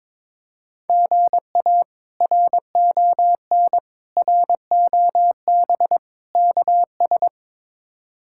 Morse code, heard as a keyed tone, about 22 wpm, 700 Hz. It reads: GA RON ROB KS
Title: K1USN SST QSO Practice - 22wpm